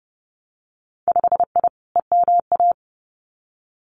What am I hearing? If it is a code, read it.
Morse code, heard as a keyed tone, about 30 wpm, 700 Hz. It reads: 5I EMA
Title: Winter Field Day Exchanges Encore 30wpm